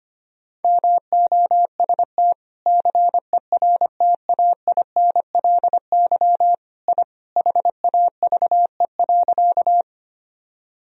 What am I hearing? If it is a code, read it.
Morse code, heard as a keyed tone, about 25 wpm, 700 Hz. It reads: MOST CERTAINLY I HAVE.
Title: Sentences from Top 1000 Words - Review Part 1 - 25wpm